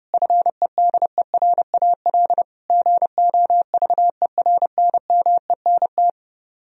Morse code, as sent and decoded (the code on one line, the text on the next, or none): ..-. -.. .-. .- .-.. / --. --- ...- . .-. -. -- . -. -
FEDERAL GOVERNMENT